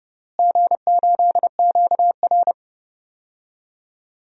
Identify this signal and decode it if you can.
Morse code, heard as a keyed tone, about 30 wpm, 700 Hz. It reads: G8QR
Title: DXCC-100 Call Signs - 4 Characters - Repeated 3x - 30wpm